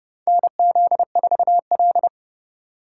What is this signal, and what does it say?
Morse code, heard as a keyed tone, about 30 wpm, 700 Hz. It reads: NZ4L
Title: USA Call Signs - 4 Characters - 2x Character Spacing - 30wpm